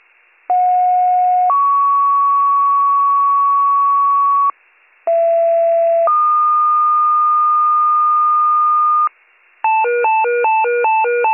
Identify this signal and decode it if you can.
Two-tone page, heard as a one-second tone, then a three-second tone, about 730 then 1120 Hz, about 670 then 1220 Hz. It is heard from 0.5 to 9.1 s.